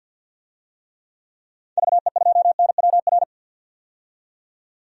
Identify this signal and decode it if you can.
Morse code, heard as a keyed tone, about 51 wpm, 700 Hz. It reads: VE2NWR
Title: DXCC-100 Call Signs - 6 Characters - 50wpm